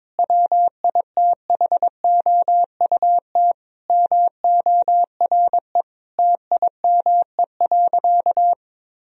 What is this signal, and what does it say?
Morse code, heard as a keyed tone, about 22 wpm, 700 Hz. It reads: WITHOUT MORE TIME.
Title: Sentences from Top 1000 Words - Review Part 3 - 22wpm